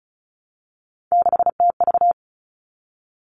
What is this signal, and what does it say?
Morse code, heard as a keyed tone, about 34 wpm, 700 Hz. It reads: 6TV
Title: All Cut Numbers - 3-Digit 35wpm